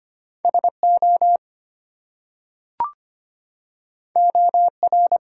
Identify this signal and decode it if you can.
Morse code, heard as a keyed tone, about 25 wpm, 700 Hz. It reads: SO OR
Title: Two Letter Words - 25wpm